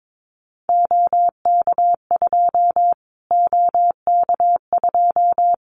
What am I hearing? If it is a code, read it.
Morse code, heard as a keyed tone, about 22 wpm, 700 Hz. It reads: OK2 OK2
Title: Callsign Prefixes for Top 25 DXCC - Repeated 3x - 22wpm